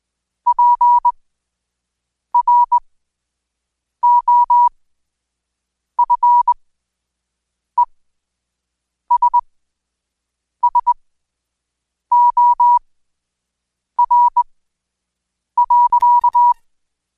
Morse code, heard as a keyed tone, about 21 wpm, spaced as for 6 wpm, 970 Hz. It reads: PROFESSOR.
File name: Code Quick/AudioDownld2/Seg109-Day14-DayDrill 14